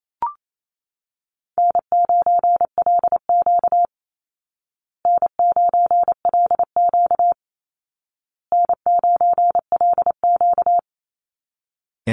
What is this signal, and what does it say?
Morse code, heard as a keyed tone, about 28 wpm, 700 Hz. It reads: N9LQ N9LQ N9LQ
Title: Easy Call Signs Repeated 3x - 28wpm